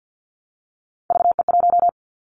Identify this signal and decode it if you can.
Morse code, heard as a keyed tone, about 50 wpm, 700 Hz. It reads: VE1
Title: Callsign Prefixes for Top 100 DXCC - 50wpm